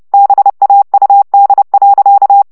Morse code, heard as a keyed tone, about 30 wpm, 800 Hz. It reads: BAUD.
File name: szamolo / CW 25 baud